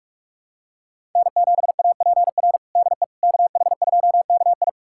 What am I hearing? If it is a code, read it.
Morse code, heard as a keyed tone, about 46 wpm, 700 Hz. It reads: N7APR DE KH2KI
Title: USA Callsign DE Callsign - 5 Characters - 45wpm